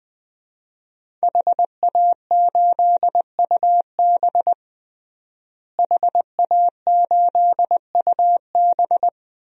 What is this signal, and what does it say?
Morse code, heard as a keyed tone, about 20 wpm, 700 Hz. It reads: HA8UB HA8UB
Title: DXCC-100 Call Signs - 5 Characters - Repeated 3x - 20wpm